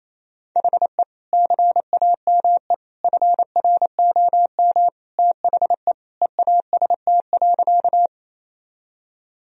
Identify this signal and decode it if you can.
Morse code, heard as a keyed tone, about 28 wpm, 690 Hz. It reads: HE CAME FROM THE EAST.
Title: Sentences from Top 1000 Words - Review Part 3 - 28wpm